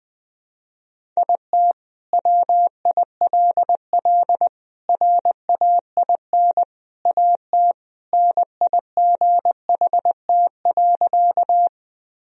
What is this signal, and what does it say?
Morse code, heard as a keyed tone, about 20 wpm, 700 Hz. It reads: IT WILL RAIN AT NIGHT.
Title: Sentences from Top 1000 Words - Review Part 1 - 20wpm